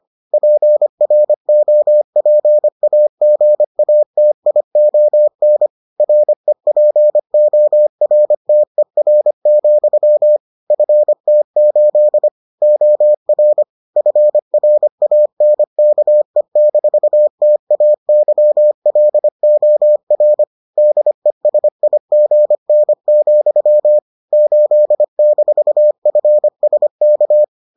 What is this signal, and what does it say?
Morse code, heard as a keyed tone, about 25 wpm, 600 Hz. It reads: PROPAGATION REPORTER, FT8 OR FRANKE-TAYLOR DESIGN, 8-FSK